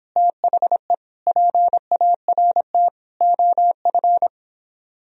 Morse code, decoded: THE PART OF

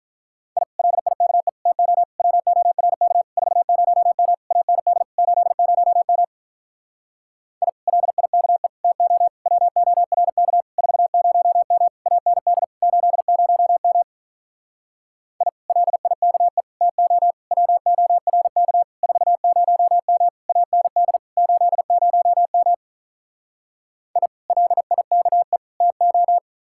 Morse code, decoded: I LIKE TO WORK 40M AND 80M I LIKE TO WORK 40M AND 80M I LIKE TO WORK 40M AND 80M I LIKE TO

